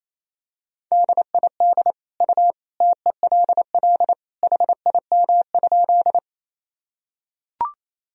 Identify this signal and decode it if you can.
Morse code, heard as a keyed tone, about 28 wpm, 700 Hz. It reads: DID U TELL HIM?
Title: Sentences from Top 200 Words and Abbreviations 28wpm